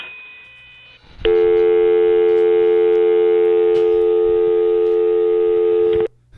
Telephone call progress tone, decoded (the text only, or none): dial tone